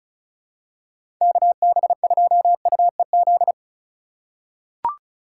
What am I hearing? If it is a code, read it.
Morse code, heard as a keyed tone, about 35 wpm, 700 Hz. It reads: KB2UEZ
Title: USA Call Signs - 6 Characters - 3x Character Spacing - 35wpm